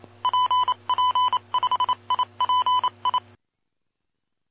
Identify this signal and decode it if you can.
Morse code, heard as a keyed tone, about 28 wpm, 1000 Hz. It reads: PP5IPI
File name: pp5bar / aulas de telegrafia / Aula-14